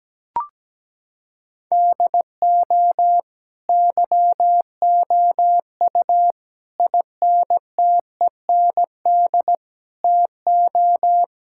Morse code, heard as a keyed tone, about 17 wpm, 700 Hz. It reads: DO YOU INTEND TO
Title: Sentences from Top 1600 Words 17wpm